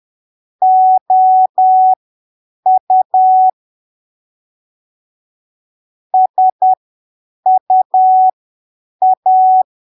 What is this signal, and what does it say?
Morse code, heard as a keyed tone, about 10 wpm, spaced as for 7 wpm, 750 Hz. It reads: OU SUA
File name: arquivos / texto / texto 28 10ppm